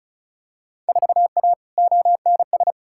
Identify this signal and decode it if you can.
Morse code, heard as a keyed tone, about 36 wpm, 700 Hz. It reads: 4A ONS
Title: Field Day Exchanges - 35wpm